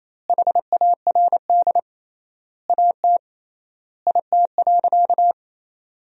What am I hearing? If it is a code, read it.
Morse code, heard as a keyed tone, about 28 wpm, 700 Hz. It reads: HARD AT IT.
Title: 3x Word-Spacing - Sentences from Top 300 Words - 28wpm